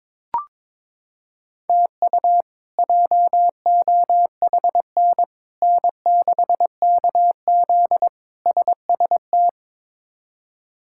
Morse code, heard as a keyed tone, about 22 wpm, 700 Hz. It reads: TU JOHN N6KZ SST
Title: K1USN SST QSO Practice - 22wpm